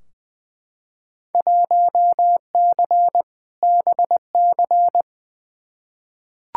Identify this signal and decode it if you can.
Morse code, heard as a keyed tone, about 20 wpm, 700 Hz. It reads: 1C BC
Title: Field Day Exchanges Encore - 20wpm